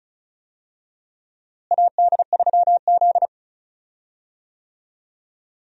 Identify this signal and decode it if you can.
Morse code, heard as a keyed tone, about 36 wpm, 700 Hz. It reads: AD3Z
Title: USA Call Signs - 4 Characters - Repeated 2x - 35wpm